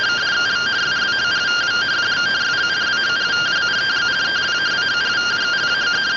Two-tone teleprinter signal, about 170 Hz shift.